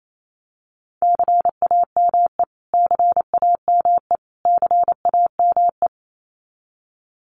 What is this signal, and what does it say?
Morse code, heard as a keyed tone, about 28 wpm, 700 Hz. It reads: CAME CAME CAME
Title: Top 500 Words Repeated 28wpm